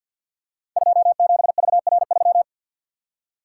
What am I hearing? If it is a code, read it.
Morse code, heard as a keyed tone, about 48 wpm, 700 Hz. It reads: 27VR3
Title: Code Groups - 5 Letter:Numbers - 50wpm